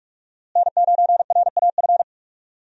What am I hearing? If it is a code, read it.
Morse code, heard as a keyed tone, about 46 wpm, 700 Hz. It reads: N9RAF
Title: USA Call Signs - 5 Characters - Repeated 2x - 45wpm